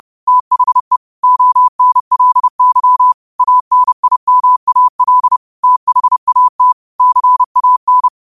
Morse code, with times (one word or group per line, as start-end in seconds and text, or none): 0.3-1.0 THE
1.2-3.1 ONLY
3.4-5.4 ANIMAL
5.6-6.7 THAT
7.0-8.1 CAN